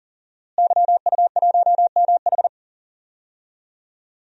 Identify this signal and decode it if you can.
Morse code, heard as a keyed tone, about 39 wpm, 700 Hz. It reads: YU1MH
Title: DXCC-100 Call Signs - 5 Characters - Repeated 3x - 40wpm